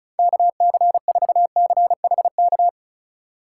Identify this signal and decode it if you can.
Morse code, heard as a keyed tone, about 36 wpm, 700 Hz. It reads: KC4CHK
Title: USA Call Signs - 2x Character Spacing - 35wpm